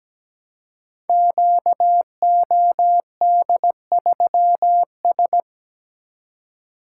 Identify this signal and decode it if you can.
Morse code, heard as a keyed tone, about 17 wpm, 700 Hz. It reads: QOD3S
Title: Code Groups - 5 Letter:Numbers - 17wpm